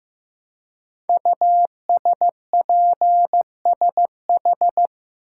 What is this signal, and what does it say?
Morse code, heard as a keyed tone, about 15 wpm, 700 Hz. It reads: USPSH
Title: Code Groups - 5 Letters - 15wpm